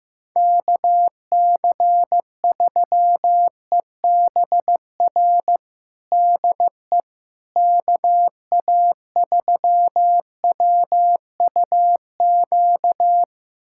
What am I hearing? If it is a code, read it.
Morse code, heard as a keyed tone, about 16 wpm, 700 Hz. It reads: KC3EBR DE KA3WUQ
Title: USA Callsign DE Callsign - 6 Characters - 15wpm